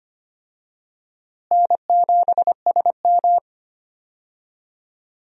Morse code, heard as a keyed tone, about 25 wpm, 700 Hz. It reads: N7SM
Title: USA Call Signs - 4 Characters - Repeated 2x - 25wpm